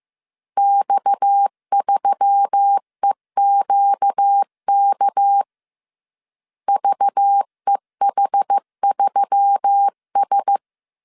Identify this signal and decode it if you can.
Morse code, heard as a keyed tone, about 15 wpm, 800 Hz. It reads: X3EQK VEH3S
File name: CW / DET / 1 6 1500C S C 14wpm 2951